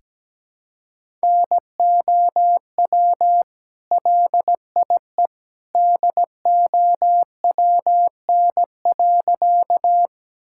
Morse code, heard as a keyed tone, about 17 wpm, 700 Hz. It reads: NOW LIE DOWN.
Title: Sentences from Top 1600 Words 17wpm